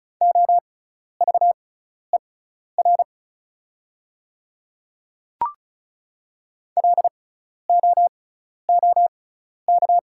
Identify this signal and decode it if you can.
Morse code, heard as a keyed tone, about 36 wpm, spaced as for 12 wpm, 700 Hz. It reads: OVER LOOK
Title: 6x Character Spacing - Top 200 Words - 35wpm